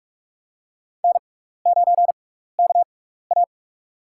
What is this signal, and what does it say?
Morse code, heard as a keyed tone, about 46 wpm, spaced as for 16 wpm, 700 Hz. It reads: N9KA
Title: USA Call Signs - 4 Characters - 6x Character Spacing - 45wpm